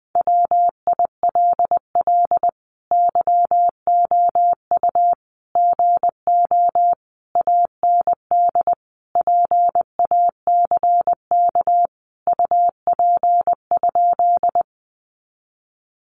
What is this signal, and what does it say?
Morse code, heard as a keyed tone, about 20 wpm, 700 Hz. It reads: WILL YOU GO AND PACK UP?
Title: Sentences from Top 1300 Words 20wpm